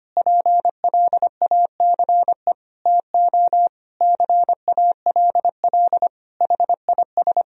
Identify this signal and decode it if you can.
Morse code, heard as a keyed tone, about 25 wpm, 700 Hz. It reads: PLACE TO CALL HIS
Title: Sentences from Top 1000 Words - Review Part 2 - 25wpm